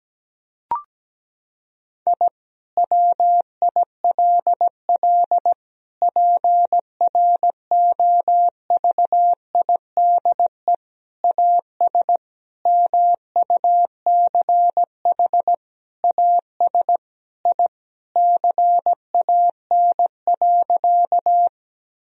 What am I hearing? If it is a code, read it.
Morse code, heard as a keyed tone, about 17 wpm, 700 Hz. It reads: I WILL PROVIDE AS MUCH AS I CAN.
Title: Sentences from Top 1000 Words - Review Part 3 - 17wpm